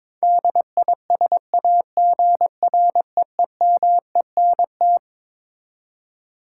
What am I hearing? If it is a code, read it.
Morse code, heard as a keyed tone, about 22 wpm, 700 Hz. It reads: DISAGREEMENT
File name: Twelve Letter Words - 22wpm